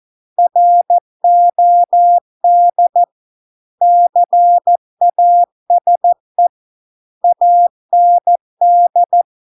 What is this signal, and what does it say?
Morse code, heard as a keyed tone, about 14 wpm, 700 Hz. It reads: ROD CASE AND